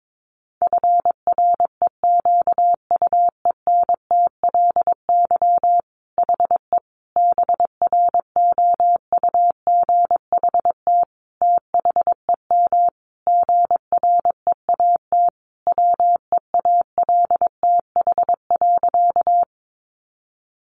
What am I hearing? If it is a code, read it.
Morse code, heard as a keyed tone, about 22 wpm, 700 Hz. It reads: FREQUENTLY HE BROUGHT THEM GREAT WEALTH.
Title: Sentences from Top 1300 Words 22wpm